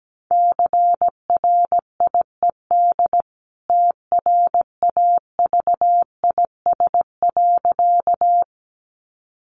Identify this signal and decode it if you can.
Morse code, heard as a keyed tone, about 17 wpm, 700 Hz. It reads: CRIED TRAVIS.
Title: A Sound Of Thunder 17wpm